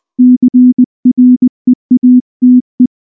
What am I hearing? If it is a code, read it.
Morse code, heard as a keyed tone, about 19 wpm, 250 Hz. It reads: CREATE
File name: morse / morse-training-20WPM-9